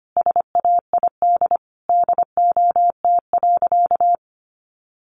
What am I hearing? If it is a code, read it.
Morse code, heard as a keyed tone, about 25 wpm, 700 Hz. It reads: SAID DOT.
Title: Sentences from Top 1500 Words 25wpm